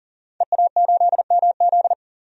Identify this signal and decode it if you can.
Morse code, heard as a keyed tone, about 37 wpm, 700 Hz. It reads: EA8MZ
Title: DXCC-100 Call Signs - 5 Characters - 40wpm